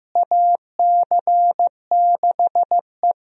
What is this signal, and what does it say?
Morse code, heard as a keyed tone, about 15 wpm, 700 Hz. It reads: AC6E
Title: USA Call Signs - 4 Characters - Repeated 2x - 15wpm